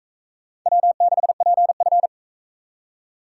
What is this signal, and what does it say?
Morse code, heard as a keyed tone, about 41 wpm, 700 Hz. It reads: W6PF